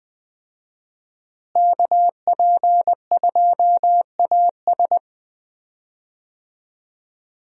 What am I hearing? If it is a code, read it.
Morse code, heard as a keyed tone, about 20 wpm, 700 Hz. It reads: KP2AS
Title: USA Call Signs - 5 Characters - Repeated 3x - 20wpm